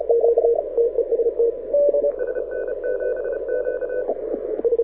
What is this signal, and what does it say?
Morse code, heard as a keyed tone, about 33 wpm, 1500 Hz, with radio noise. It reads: SN7Q
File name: audio zawody / IARU 2015 CW SN7Q